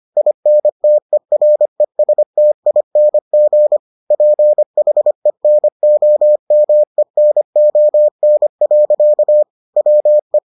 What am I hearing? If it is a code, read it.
Morse code, heard as a keyed tone, about 25 wpm, 600 Hz. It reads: INTERESTING PHENOMENON. WE